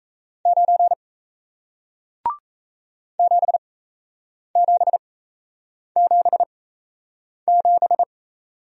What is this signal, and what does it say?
Morse code, heard as a keyed tone, about 40 wpm, spaced as for 8 wpm, 700 Hz. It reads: 9 7777